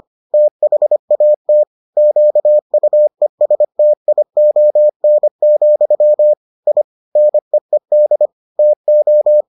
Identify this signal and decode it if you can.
Morse code, heard as a keyed tone, about 25 wpm, 600 Hz. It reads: THAT QUESTION, I NEED TO